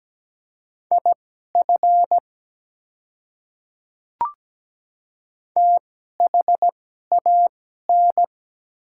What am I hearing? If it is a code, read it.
Morse code, heard as a keyed tone, about 17 wpm, spaced as for 12 wpm, 700 Hz. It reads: IF THAN
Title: 2x Character Spacing - Top 100 Words - 17wpm